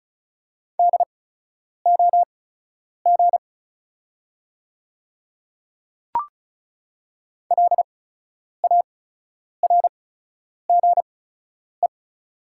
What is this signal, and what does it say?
Morse code, heard as a keyed tone, about 36 wpm, spaced as for 10 wpm, 700 Hz. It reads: DOG LARGE
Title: 8x Character Spacing - Top 500 Words - 35wpm